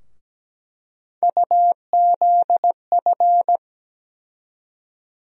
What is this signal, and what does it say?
Morse code, heard as a keyed tone, about 17 wpm, 700 Hz. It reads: UZF
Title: USA Call Sign Postfixes - 3 Letters 17wpm